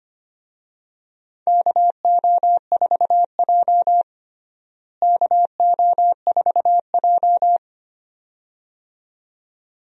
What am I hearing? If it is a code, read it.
Morse code, heard as a keyed tone, about 25 wpm, 700 Hz. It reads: KO4J KO4J
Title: USA Call Signs - 4 Characters - Repeated 2x - 25wpm